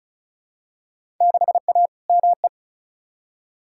Morse code, heard as a keyed tone, about 36 wpm, 700 Hz. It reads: 6A ME
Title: Field Day Exchanges - 35wpm